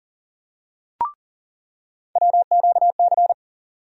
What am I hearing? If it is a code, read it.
Morse code, heard as a keyed tone, about 40 wpm, 700 Hz. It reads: WQC
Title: USA Call Sign Postfixes - 3 Letters 40wpm